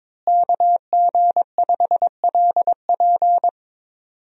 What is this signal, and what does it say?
Morse code, heard as a keyed tone, about 22 wpm, 700 Hz. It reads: KG5LP
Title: USA Call Signs - 5 Characters - Repeated 3x - 22wpm